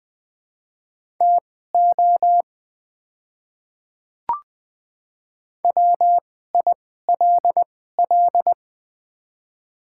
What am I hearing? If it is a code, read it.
Morse code, heard as a keyed tone, about 20 wpm, spaced as for 14 wpm, 700 Hz. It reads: TO WILL
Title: 2x Character Spacing - Top 100 Words - 20wpm